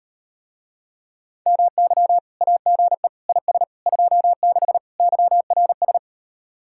Morse code, heard as a keyed tone, about 39 wpm, 700 Hz. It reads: MY AGE IS 26 YRS